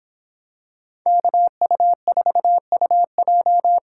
Morse code, heard as a keyed tone, about 26 wpm, 700 Hz. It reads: KU4UJ